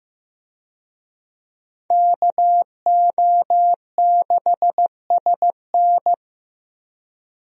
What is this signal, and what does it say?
Morse code, heard as a keyed tone, about 15 wpm, 700 Hz. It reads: KO6SN
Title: USA Call Signs - 5 Characters - Repeated 3x - 15wpm